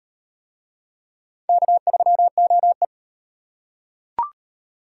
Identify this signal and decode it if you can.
Morse code, heard as a keyed tone, about 36 wpm, 700 Hz. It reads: K3OE